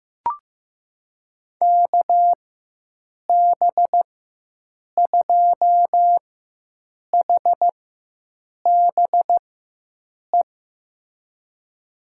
Morse code, heard as a keyed tone, about 15 wpm, spaced as for 7 wpm, 700 Hz. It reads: KB2HBE
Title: USA Call Signs - 6 Characters - 4x Character Spacing - 15wpm